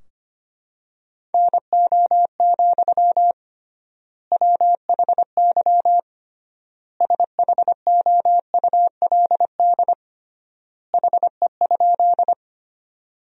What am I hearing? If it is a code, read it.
Morse code, heard as a keyed tone, about 25 wpm, 700 Hz. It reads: NO, WHY SHOULD HE?
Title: 3x Word-Spacing - Sentences from Top 300 Words - 25wpm